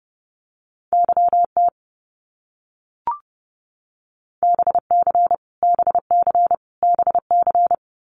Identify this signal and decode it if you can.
Morse code, heard as a keyed tone, about 32 wpm, 700 Hz. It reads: YT BC BC BC
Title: Canadian Provinces and Territories - Repeated 3x - 30wpm